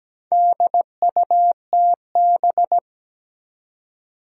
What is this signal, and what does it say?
Morse code, heard as a keyed tone, about 17 wpm, 700 Hz. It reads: DUTB
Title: Four Challenging Letter-Number Combinations 17wpm